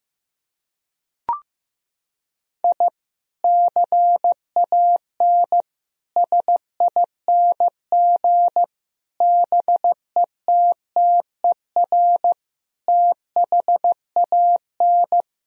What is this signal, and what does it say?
Morse code, heard as a keyed tone, about 16 wpm, 700 Hz. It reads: I CAN SING BETTER THAN
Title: Sentences from Top 1000 Words - Review Part 2 - 15wpm